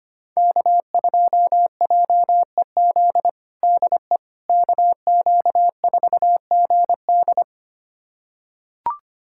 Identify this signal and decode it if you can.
Morse code, heard as a keyed tone, about 25 wpm, 700 Hz. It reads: K2JEZ DE KQ4GD
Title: USA Callsign DE Callsign - 5 Characters - 25wpm